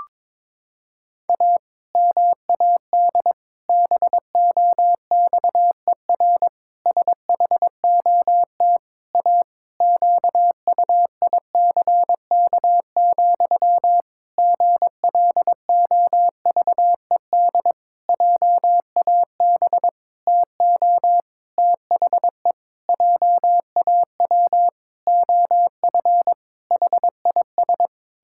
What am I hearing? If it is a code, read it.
Morse code, heard as a keyed tone, about 22 wpm, 700 Hz. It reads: A MAD BOXER SHOT A QUICK, GLOVED JAB TO THE JAW OF HIS
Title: Pangram Challenge v2 - 22wpm